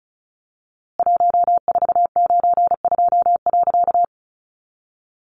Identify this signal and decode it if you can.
Morse code, heard as a keyed tone, about 34 wpm, 700 Hz. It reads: 1492.
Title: A Sound Of Thunder 35wpm